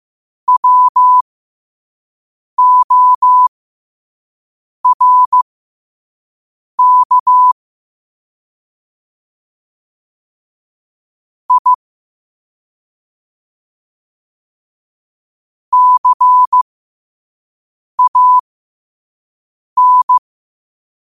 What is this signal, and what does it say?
Morse code, heard as a keyed tone, about 15 wpm, spaced as for 5 wpm, 1000 Hz. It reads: WORK I CAN
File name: qotdfiles / 5wpm